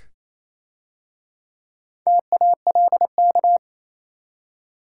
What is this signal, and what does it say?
Morse code, heard as a keyed tone, about 27 wpm, 690 Hz. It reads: TALK
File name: Easy Words - Speed-Racing with Hints - 22wpm